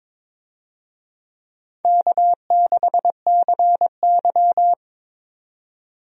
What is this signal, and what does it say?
Morse code, heard as a keyed tone, about 22 wpm, 700 Hz. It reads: K6CY
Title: USA Call Signs - 4 Characters - 22wpm